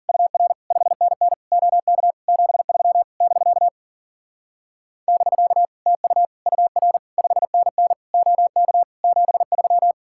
Morse code, heard as a keyed tone, about 54 wpm, 700 Hz. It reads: UR 5NN OK 73 <BK> <BK> TU UR 5NN OK 73